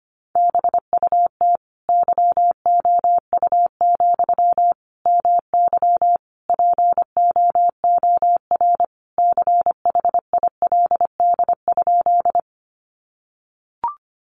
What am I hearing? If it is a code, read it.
Morse code, heard as a keyed tone, about 25 wpm, 700 Hz. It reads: BUT YOU, MY POOR CHILD?
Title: Sentences from Top 1000 Words 25wpm